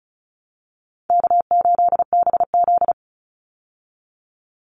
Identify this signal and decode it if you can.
Morse code, heard as a keyed tone, about 34 wpm, 700 Hz. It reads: K8BZ
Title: Easy Call Signs Repeated 3x - 35wpm